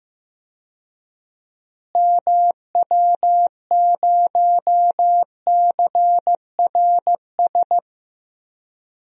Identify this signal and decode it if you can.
Morse code, heard as a keyed tone, about 15 wpm, 700 Hz. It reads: MW0CRS
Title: DXCC-100 Call Signs - 6 Characters - Repeated 2x - 15wpm